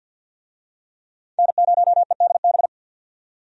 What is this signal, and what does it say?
Morse code, heard as a keyed tone, about 50 wpm, 700 Hz. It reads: N0EDB
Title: USA Call Signs - 5 Characters - 50wpm